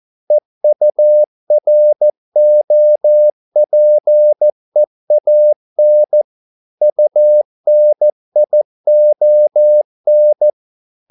Morse code, heard as a keyed tone, about 14 wpm, 600 Hz. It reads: EUROPEAN UNION